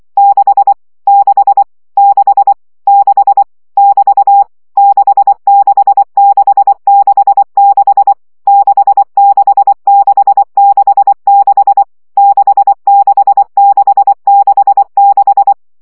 Morse code, heard as a keyed tone, about 24 wpm, 800 Hz. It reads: 6 6 6 6 <BT> 66666 66666 66666
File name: CW gyakorlas 6 20 baud